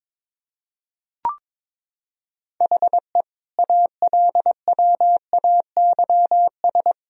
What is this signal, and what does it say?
Morse code, heard as a keyed tone, about 22 wpm, 700 Hz. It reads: HE ALWAYS
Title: Sentences from Top 1000 Words - Review Part 3 - 22wpm